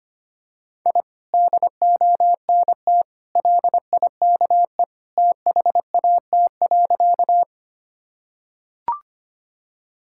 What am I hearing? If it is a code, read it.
Morse code, heard as a keyed tone, about 25 wpm, 700 Hz. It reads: I DONT LIKE THAT.
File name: practice / 25wpm / Sentences with 2-Word Phrases - 25wpm